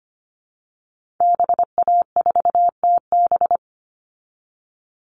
Morse code, heard as a keyed tone, about 25 wpm, 700 Hz. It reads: BA4TB
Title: Call Signs Repeated 2x - 25wpm